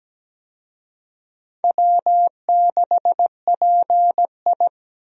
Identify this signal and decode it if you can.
Morse code, heard as a keyed tone, about 17 wpm, 700 Hz. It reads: W6PI
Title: USA Call Signs - 4 Characters - 17wpm